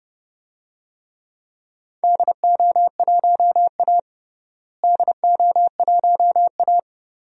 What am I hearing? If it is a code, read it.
Morse code, heard as a keyed tone, about 30 wpm, 700 Hz. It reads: DO1A DO1A
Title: DXCC-100 Call Signs - 4 Characters - Repeated 3x - 30wpm